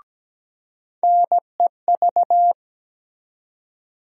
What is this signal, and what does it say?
Morse code, heard as a keyed tone, about 17 wpm, 700 Hz. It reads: NEV
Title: USA Call Sign Postfixes - 3 Letters 17wpm